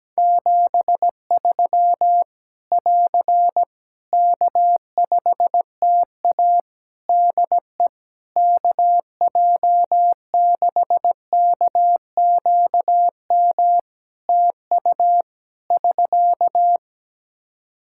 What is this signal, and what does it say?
Morse code, heard as a keyed tone, about 17 wpm, 700 Hz. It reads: 73 <AR> K5TA DE KJ6KQM TU <SK>
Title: Practice QSO - Exchange 4 - Ending - 17wpm